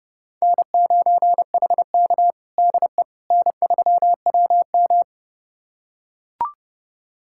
Morse code, heard as a keyed tone, about 30 wpm, 700 Hz. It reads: N9HK DE N3WM
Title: USA Callsign DE Callsign - 4 Characters - 30wpm